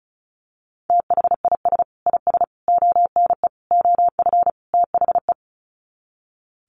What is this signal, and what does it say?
Morse code, heard as a keyed tone, about 34 wpm, 700 Hz. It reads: THIS IS ONE OF THE
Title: Sets of 5 Words 35wpm